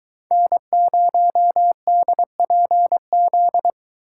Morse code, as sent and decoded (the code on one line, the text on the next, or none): -. ----- -.. .--. --..
N0DPZ